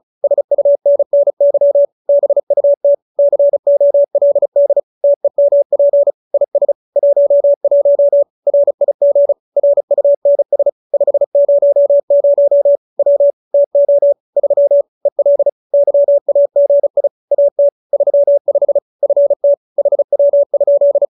Morse code, heard as a keyed tone, about 34 wpm, 570 Hz. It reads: SUNNY BUT COLD TEMP IS 11 RIG RUNS 500 W TO 3 EL YAGI AT 35 FT HW?